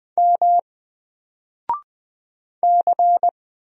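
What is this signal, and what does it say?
Morse code, heard as a keyed tone, about 20 wpm, 700 Hz. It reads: M C
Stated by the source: Single letter and number Morse drill at 20 wpm, from N4LCD.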